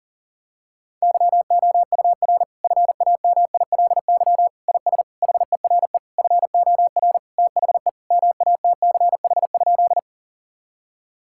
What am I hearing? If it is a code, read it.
Morse code, heard as a keyed tone, about 40 wpm, 700 Hz. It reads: YOUR FAMILY IS HERE FOR THE MATCH?
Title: Sentences from Top 1600 Words 40wpm